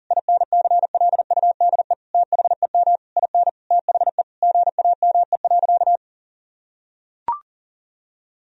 Morse code, heard as a keyed tone, about 40 wpm, 700 Hz. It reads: INCLUDE THEM IN THE GAME.
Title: Sentences from Top 1000 Words - Review Part 3 - 40wpm